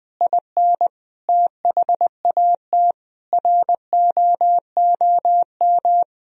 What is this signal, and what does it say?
Morse code, heard as a keyed tone, about 20 wpm, 700 Hz. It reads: IN THAT ROOM
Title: Sentences from Top 500 Words and Abbreviations 20wpm